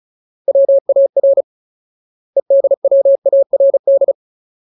Morse code, heard as a keyed tone, about 34 wpm, 550 Hz. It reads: WAR EDWARD
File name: Ultimate Word List / 35WPM Ultimate